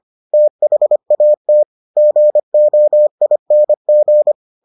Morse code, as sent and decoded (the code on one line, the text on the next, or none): - .... .- - / --. --- .. -. --.
THAT GOING